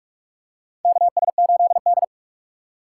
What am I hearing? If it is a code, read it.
Morse code, heard as a keyed tone, about 41 wpm, 700 Hz. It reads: KS8D
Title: USA Call Signs - 4 Characters - 6x Character Spacing - 45wpm